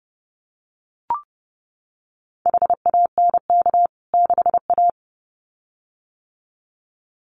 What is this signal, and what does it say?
Morse code, heard as a keyed tone, about 30 wpm, 700 Hz. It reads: HANK 6A
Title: CWT Exchange 30wpm